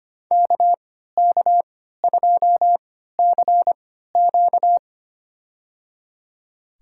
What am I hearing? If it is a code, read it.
Morse code, heard as a keyed tone, about 25 wpm, spaced as for 14 wpm, 700 Hz. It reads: KK2CQ
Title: USA Call Signs - 5 Characters - 3x Character Spacing - 25wpm